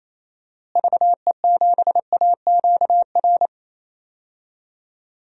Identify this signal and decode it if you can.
Morse code, heard as a keyed tone, about 28 wpm, 700 Hz. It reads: VE7AQR
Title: DXCC-100 Call Signs - 6 Characters - Repeated 2x - 28wpm